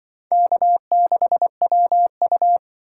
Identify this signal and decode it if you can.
Morse code, heard as a keyed tone, about 25 wpm, 700 Hz. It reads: K6WU
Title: Speed Racing - USA 1x2 Call Signs 20wpm